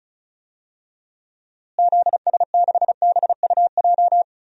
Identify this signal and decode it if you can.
Morse code, heard as a keyed tone, about 36 wpm, 700 Hz. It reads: ZS6BUJ
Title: DXCC-100 Call Signs - 6 Characters - Repeated 3x - 35wpm